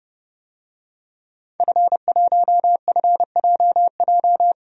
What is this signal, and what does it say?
Morse code, heard as a keyed tone, about 31 wpm, 700 Hz. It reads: F1FJJ
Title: DXCC-100 Call Signs - 5 Characters - 30wpm